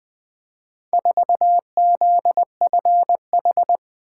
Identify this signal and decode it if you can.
Morse code, heard as a keyed tone, about 20 wpm, 700 Hz. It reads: 4ZFH
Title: Four Letter-Number Combinations - 20wpm